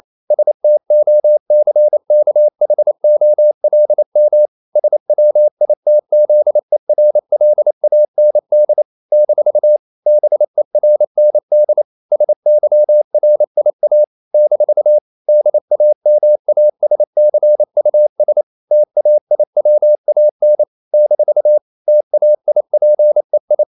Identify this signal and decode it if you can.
Morse code, heard as a keyed tone, about 28 wpm, 600 Hz. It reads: STOCKHOLM SWITZERLAND - BERND SYRIA - DAMASCUS TAIWAN - TAIPEI
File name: CW / Practice / NationalCapitals28